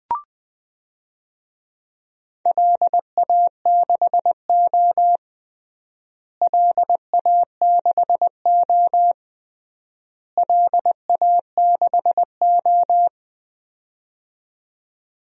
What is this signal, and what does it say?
Morse code, heard as a keyed tone, about 20 wpm, 700 Hz. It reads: LA6O LA6O LA6O
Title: DXCC-100 Call Signs - 4 Characters - Repeated 3x - 20wpm